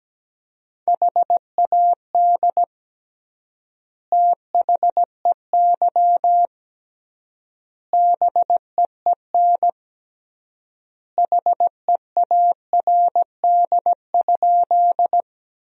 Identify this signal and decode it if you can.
Morse code, heard as a keyed tone, about 17 wpm, 700 Hz. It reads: HAD THEY BEEN HEARD?
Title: 3x Word-Spacing - Sentences from Top 400 Words - 17wpm